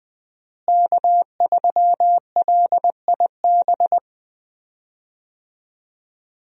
Morse code, heard as a keyed tone, about 20 wpm, 700 Hz. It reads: K3LIB
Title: USA Call Signs - 5 Characters - Repeated 3x - 20wpm